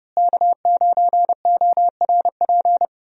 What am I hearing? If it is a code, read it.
Morse code, heard as a keyed tone, about 31 wpm, 700 Hz. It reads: K9ORP
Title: USA Call Signs - 5 Characters - Repeated 2x - 30wpm